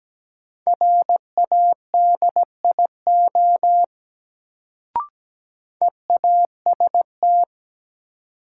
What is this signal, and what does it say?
Morse code, heard as a keyed tone, about 17 wpm, 700 Hz. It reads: RADIO EAST
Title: Sentences from Top 1000 Words - Review Part 3 - New Word Drill - 17wpm